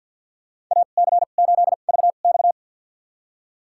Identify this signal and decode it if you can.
Morse code, heard as a keyed tone, about 49 wpm, spaced as for 35 wpm, 700 Hz. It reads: AC8VX
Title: USA Call Signs - 5 Characters - 2x Character Spacing - 50wpm